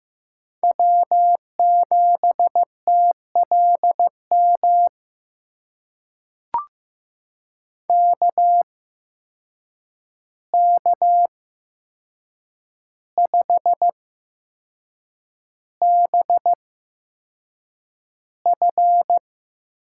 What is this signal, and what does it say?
Morse code, heard as a keyed tone, about 15 wpm, spaced as for 6 wpm, 700 Hz. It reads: W7TLM KK5BF
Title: USA Call Signs - 5 Characters - 8x Character Spacing - 15wpm